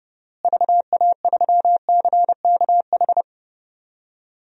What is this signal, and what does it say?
Morse code, heard as a keyed tone, about 32 wpm, 700 Hz. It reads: VA3CKH
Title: DXCC-100 Call Signs - 6 Characters - Repeated 3x - 30wpm